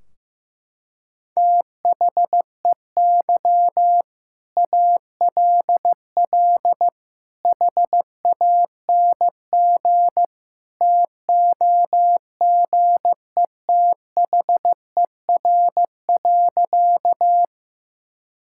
Morse code, heard as a keyed tone, about 15 wpm, 700 Hz. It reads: THEY ALL HANG TOGETHER.
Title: Sentences from Top 1600 Words 15wpm